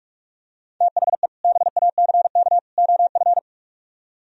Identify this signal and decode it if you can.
Morse code, heard as a keyed tone, about 44 wpm, 700 Hz. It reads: THE BACK OF